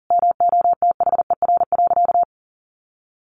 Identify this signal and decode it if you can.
Morse code, heard as a keyed tone, about 40 wpm, 700 Hz. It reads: MOTHER.
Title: Sentences from Top 1300 Words 40wpm